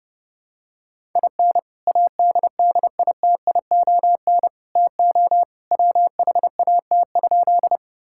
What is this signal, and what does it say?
Morse code, heard as a keyed tone, about 30 wpm, 700 Hz. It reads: IN ADDITION TO WHAT?
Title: Sentences from Top 1000 Words - Review Part 6 - 30wpm